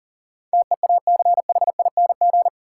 Morse code, heard as a keyed tone, about 40 wpm, 700 Hz. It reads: TEACHING
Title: Sentences from Top 1100 Words - New Word Drill Alt - 40wpm